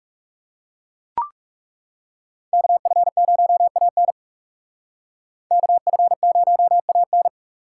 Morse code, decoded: KF0AN KF0AN

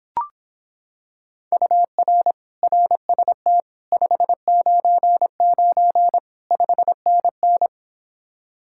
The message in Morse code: UR RST 599 5NN

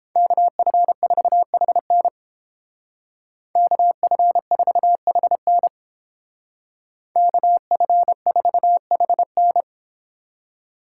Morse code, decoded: KF4HN KF4HN KF4HN

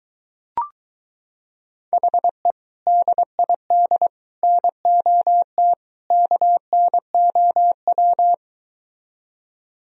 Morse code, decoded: HE DID NOT KNOW

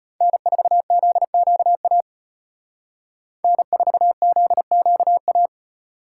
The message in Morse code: N4ZQA N4ZQA

